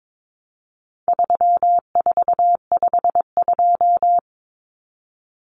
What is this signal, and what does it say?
Morse code, heard as a keyed tone, about 22 wpm, 700 Hz. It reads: 3452
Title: Numbers - 4-Digit 22wpm